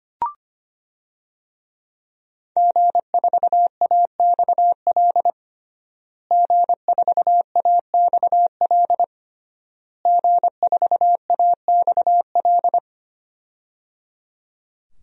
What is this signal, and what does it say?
Morse code, heard as a keyed tone, about 25 wpm, 700 Hz. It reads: G4AXL G4AXL G4AXL
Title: DXCC-100 Call Signs - 5 Characters - Repeated 3x - 25wpm